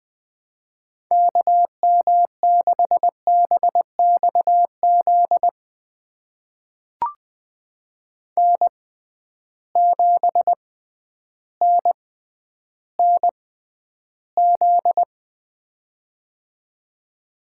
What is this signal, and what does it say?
Morse code, heard as a keyed tone, about 20 wpm, spaced as for 11 wpm, 700 Hz. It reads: KM6BXZ N7NNZ
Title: USA Call Signs - 6x Character Spacing - 20wpm